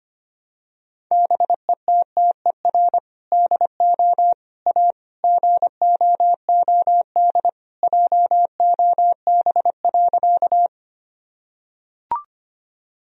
Morse code, heard as a keyed tone, about 25 wpm, 700 Hz. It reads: BETTER DO A GOOD JOB.
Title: Sentences from Top 1000 Words - Review Part 3 - 25wpm